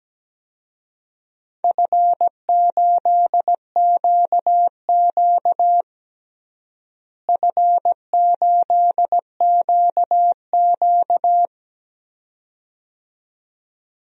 Morse code, heard as a keyed tone, about 17 wpm, 700 Hz. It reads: F8QQ F8QQ
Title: DXCC-100 Call Signs - 4 Characters - Repeated 2x - 17wpm